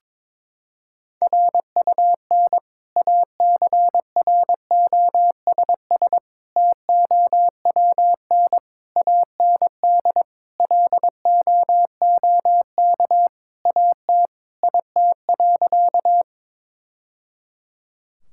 Morse code, heard as a keyed tone, about 22 wpm, 700 Hz. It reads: RUN ACROSS TOWN AND LOOK AT IT.
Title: Sentences from Top 1000 Words - Review Part 5 - 22wpm